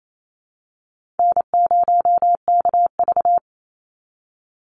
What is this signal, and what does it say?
Morse code, heard as a keyed tone, about 28 wpm, 700 Hz. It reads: N0KV
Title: Call Signs Repeated 3x - 28wpm